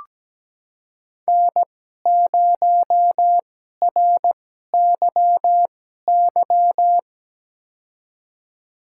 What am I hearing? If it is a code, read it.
Morse code, heard as a keyed tone, about 17 wpm, spaced as for 12 wpm, 700 Hz. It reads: N0RYY
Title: USA Call Signs - 5 Characters - 2x Character Spacing - 17wpm